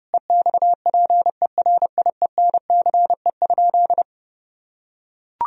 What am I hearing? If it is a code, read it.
Morse code, heard as a keyed tone, about 32 wpm, 700 Hz. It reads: EXPERIENCE?
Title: Sentences from Top 1000 Words - Review Part 4 - 30wpm